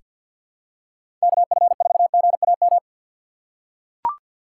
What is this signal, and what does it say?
Morse code, heard as a keyed tone, about 45 wpm, 700 Hz. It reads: KF4GAM
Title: USA Call Signs - 6 Characters - 3x Character Spacing - 50wpm